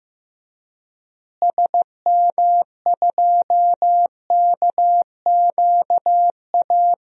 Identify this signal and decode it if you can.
Morse code, heard as a keyed tone, about 15 wpm, 700 Hz. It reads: SM2KQA
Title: DXCC-100 Call Signs - 6 Characters - Repeated 3x - 15wpm